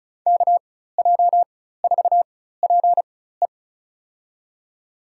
Morse code, heard as a keyed tone, about 34 wpm, spaced as for 16 wpm, 700 Hz. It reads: KJ4PE
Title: USA Call Signs - 5 Characters - 4x Character Spacing - 35wpm